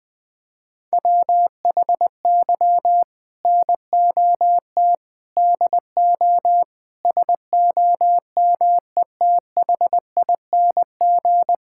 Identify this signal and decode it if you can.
Morse code, heard as a keyed tone, about 21 wpm, 700 Hz. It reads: WHY NOT DO SOMETHING
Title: Sentences from Top 1000 Words - Review Part 6 - 20wpm